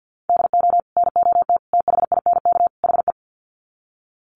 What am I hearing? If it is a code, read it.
Morse code, heard as a keyed tone, about 50 wpm, 700 Hz. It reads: DO NOT THINK HE
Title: Sets of 4 Words 50wpm